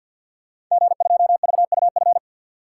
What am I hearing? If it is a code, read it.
Morse code, heard as a keyed tone, about 52 wpm, 700 Hz. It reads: G2VUF
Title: Code Groups - 5 Letter:Numbers - 50wpm